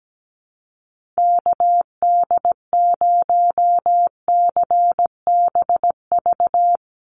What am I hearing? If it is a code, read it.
Morse code, heard as a keyed tone, about 17 wpm, 700 Hz. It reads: KD0CBV
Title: USA Call Signs 17wpm